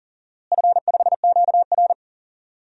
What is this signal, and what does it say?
Morse code, heard as a keyed tone, about 38 wpm, 700 Hz. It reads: F5QR
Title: DXCC-100 Call Signs - 4 Characters - Repeated 2x - 40wpm